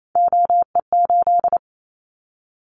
Morse code, heard as a keyed tone, about 28 wpm, 700 Hz. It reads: OE8
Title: Callsign Prefixes for Top 25 DXCC - 28wpm